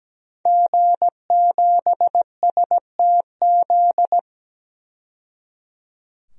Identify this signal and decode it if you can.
Morse code, heard as a keyed tone, about 17 wpm, 700 Hz. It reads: G7STZ
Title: DXCC-100 Call Signs - 5 Characters - 17wpm